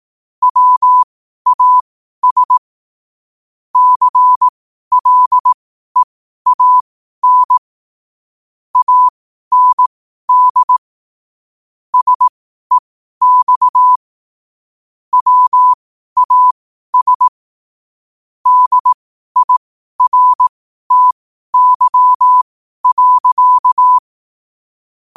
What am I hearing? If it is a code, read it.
Morse code, heard as a keyed tone, about 18 wpm, spaced as for 12 wpm, 1000 Hz. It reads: WAS CLEAN AND SEX WAS DIRTY.
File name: qotdfiles / 12wpm